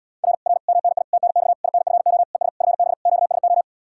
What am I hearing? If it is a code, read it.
Morse code, heard as a keyed tone, about 80 wpm, 700 Hz. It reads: R R MNI TNX INFO ES FB QSO